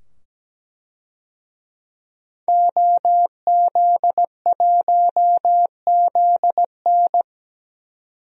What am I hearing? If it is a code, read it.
Morse code, heard as a keyed tone, about 17 wpm, 700 Hz. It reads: OZ1ZN
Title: DXCC-100 Call Signs - 5 Characters - 17wpm